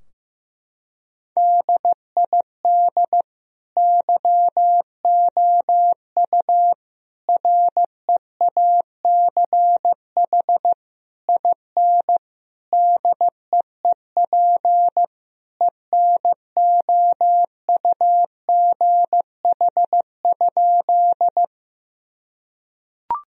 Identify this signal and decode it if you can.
Morse code, heard as a keyed tone, about 15 wpm, 700 Hz. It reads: DID YOU REACH IN DEEP ENOUGH?
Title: Sentences from Top 1000 Words - Review Part 3 - 15wpm